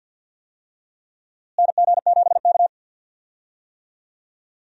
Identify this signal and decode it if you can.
Morse code, heard as a keyed tone, about 46 wpm, 700 Hz. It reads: NG7K
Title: USA Call Signs - 4 Characters - Repeated 2x - 50wpm